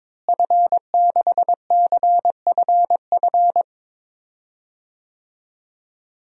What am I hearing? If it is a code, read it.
Morse code, heard as a keyed tone, about 22 wpm, 700 Hz. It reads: F6CFF
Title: DXCC-100 Call Signs - 5 Characters - Repeated 3x - 22wpm